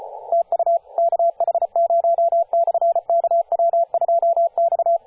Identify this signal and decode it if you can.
Morse code, heard as a keyed tone, about 33 wpm, 680 Hz, with radio noise. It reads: TU KH0/KW2X
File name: posts / 2016-sep-22-0537z-kh0-kw2x-18-077mhz-cw